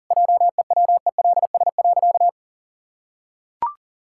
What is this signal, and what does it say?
Morse code, heard as a keyed tone, about 40 wpm, 700 Hz. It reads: JEWELS.
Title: Pangram Challenge v2 - 40wpm